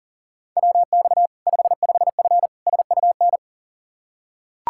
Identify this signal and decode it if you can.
Morse code, heard as a keyed tone, about 40 wpm, 700 Hz. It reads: WX 55F SUN
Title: Practice QSO - Exchange 2 - Info - 40wpm